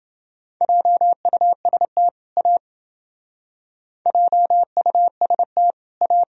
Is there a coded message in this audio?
Yes: Morse code from 0.6 to 6.2 s, heard as a keyed tone, about 29 wpm, 700 Hz. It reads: JUST A JUST A